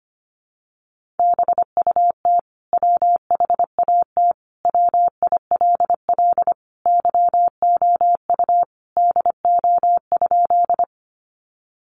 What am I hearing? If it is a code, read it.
Morse code, heard as a keyed tone, about 25 wpm, 700 Hz. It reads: BUT WHAT WILL YOU DO?
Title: Sentences from Top 100 Words 25wpm